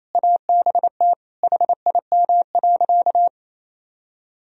Morse code, heard as a keyed tone, about 28 wpm, 700 Hz. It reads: ABT HIM.
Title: Sentences from Top 200 Words and Abbreviations 28wpm